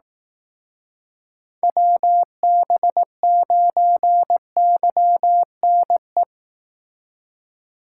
Morse code, heard as a keyed tone, about 18 wpm, 700 Hz. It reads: WB9YNE
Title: Speed Racing - USA 2x3 Call Signs 15wpm